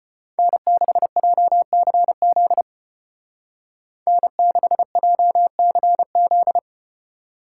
Morse code, heard as a keyed tone, about 35 wpm, 700 Hz. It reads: N6JCZ N6JCZ